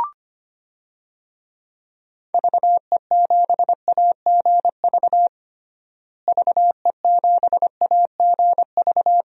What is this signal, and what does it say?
Morse code, heard as a keyed tone, about 25 wpm, 700 Hz. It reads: VE7AGV VE7AGV
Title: DXCC-100 Call Signs - 6 Characters - Repeated 3x - 25wpm